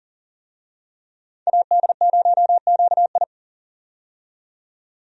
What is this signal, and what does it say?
Morse code, heard as a keyed tone, about 40 wpm, 700 Hz. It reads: AD0QI
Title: USA Call Signs - 5 Characters - 40wpm